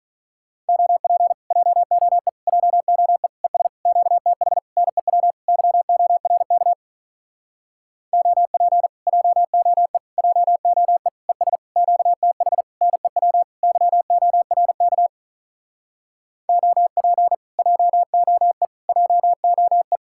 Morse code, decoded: OP JOE JOE ES QTH NEW YORK OP JOE JOE ES QTH NEW YORK OP JOE JOE

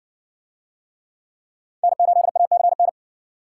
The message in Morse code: N8ACN

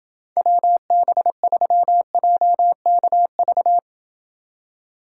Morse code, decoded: WB3JKV

